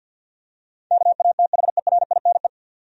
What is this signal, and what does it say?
Morse code, heard as a keyed tone, about 47 wpm, 700 Hz. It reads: KATHERINE
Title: USA Common Names - 50wpm